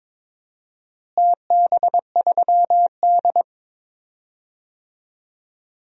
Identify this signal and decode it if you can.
Morse code, heard as a keyed tone, about 22 wpm, 700 Hz. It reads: TB3D
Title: DXCC-100 Call Signs - 4 Characters - Repeated 2x - 22wpm